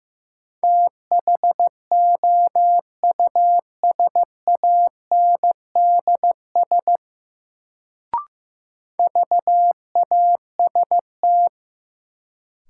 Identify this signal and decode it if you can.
Morse code, heard as a keyed tone, about 15 wpm, 700 Hz. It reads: THOUSANDS VAST
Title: Sets of 4 Words - New Word Drill - 15wpm